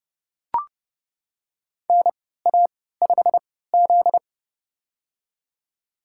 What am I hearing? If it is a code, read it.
Morse code, heard as a keyed tone, about 30 wpm, spaced as for 17 wpm, 700 Hz. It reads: NA5Z
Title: USA Call Signs - 4 Characters - 3x Character Spacing - 30wpm